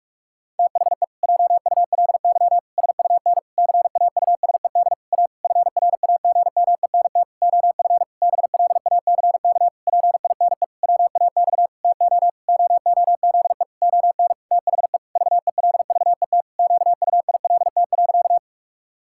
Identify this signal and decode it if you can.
Morse code, heard as a keyed tone, about 46 wpm, 700 Hz. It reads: THE JULY SUN CAUSED A FRAGMENT OF BLACK PINE WAX TO OOZE ON THE VELVET QUILT.
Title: Pangram Challenge v2 - 45wpm